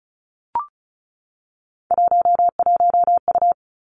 Morse code, heard as a keyed tone, about 36 wpm, 700 Hz. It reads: 11U
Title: Three Challenging Letter-Number Combinations 35wpm